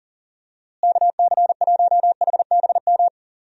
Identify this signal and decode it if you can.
Morse code, heard as a keyed tone, about 37 wpm, 700 Hz. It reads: KC1HBM
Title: USA Call Signs - 6 Characters - Repeated 3x - 40wpm